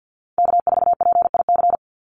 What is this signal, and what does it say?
Morse code, heard as a keyed tone, about 50 wpm, 700 Hz. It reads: K4PIC
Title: NA SOTA Chaser Callsigns 50wpm